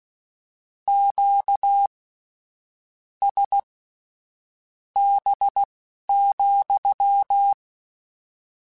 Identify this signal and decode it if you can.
Morse code, heard as a keyed tone, about 16 wpm, spaced as for 7 wpm, 800 Hz. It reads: QSB,